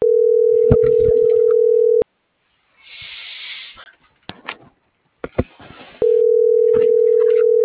Ringback tone (North America), 440 + 480 Hz.